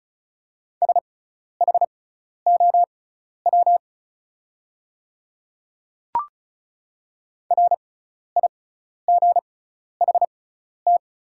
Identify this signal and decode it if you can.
Morse code, heard as a keyed tone, about 36 wpm, spaced as for 12 wpm, 690 Hz. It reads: SHOW RIGHT
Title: 6x Character Spacing - Top 200 Words - 35wpm